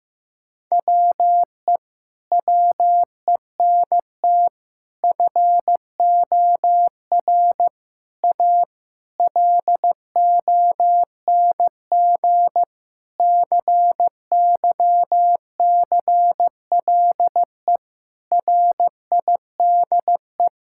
Morse code, heard as a keyed tone, about 15 wpm, 700 Hz. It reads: WE WENT FOR A LONG CYCLE RIDE
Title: Sentences from Top 1000 Words - Review Part 6 - 15wpm